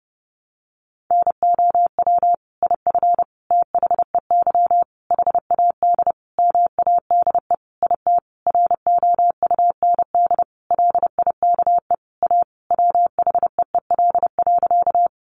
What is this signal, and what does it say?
Morse code, heard as a keyed tone, about 30 wpm, 700 Hz. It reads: NOW IF THEY HAD MADE IT ROUND LIKE A WHEEL.
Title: Sentences from Top 500 Words 30wpm